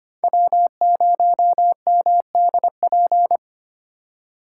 Morse code, heard as a keyed tone, about 25 wpm, 700 Hz. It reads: W0MDP
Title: USA Call Signs - 5 Characters - 3x Character Spacing - 25wpm